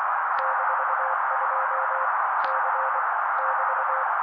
Morse code, heard as a keyed tone, about 24 wpm, 530 Hz. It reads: -2 C <BT>
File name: daily / 23wpm noise